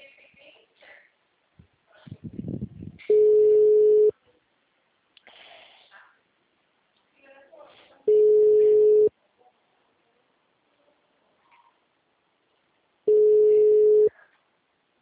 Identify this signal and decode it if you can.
Ringback tone (Europe, 425 Hz), with speech.
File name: out- -2017- - - .20957